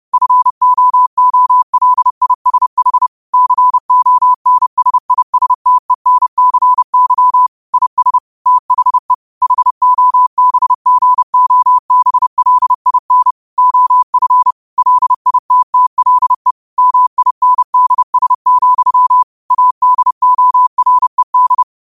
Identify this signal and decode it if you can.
Morse code, heard as a keyed tone, about 30 wpm, 1000 Hz. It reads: FOOLISH CONSISTENCY IS THE HOBGOBLIN OF LITTLE MINDS, ADORED
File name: qotdfiles / 30wpm